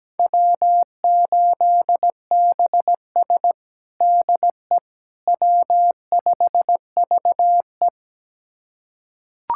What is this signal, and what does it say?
Morse code, heard as a keyed tone, about 17 wpm, 700 Hz. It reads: W8BS DE W5VE
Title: USA Callsign DE Callsign - 4 Characters - 17wpm